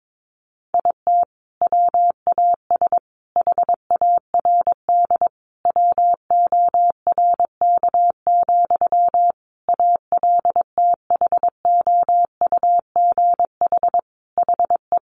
Morse code, heard as a keyed tone, about 22 wpm, 700 Hz. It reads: IT WAS HARD WORK, ALTHOUGH HE
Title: Sentences from Top 600 Words 22wpm